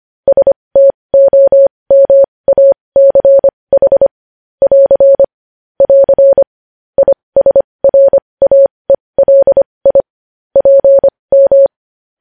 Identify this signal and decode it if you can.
Morse code, heard as a keyed tone, about 25 wpm, spaced as for 20 wpm, 550 Hz. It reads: STOMACH <AR> <AR> ISRAELI PM